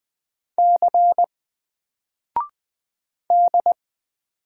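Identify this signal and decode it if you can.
Morse code, heard as a keyed tone, about 20 wpm, 700 Hz. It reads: C D
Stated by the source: Single letter and number Morse drill at 20 wpm, from N4LCD.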